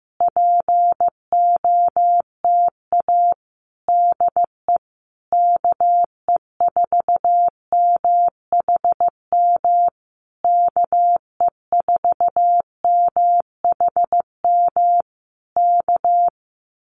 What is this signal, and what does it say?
Morse code, heard as a keyed tone, about 15 wpm, 700 Hz. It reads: POTA DE KE4MHM KE4MHM K
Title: Common QSO Phrases Encore 15wpm